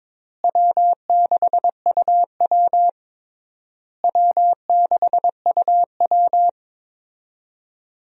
Morse code, heard as a keyed tone, about 22 wpm, 700 Hz. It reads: W6UW W6UW
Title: USA Call Signs - 4 Characters - Repeated 3x - 22wpm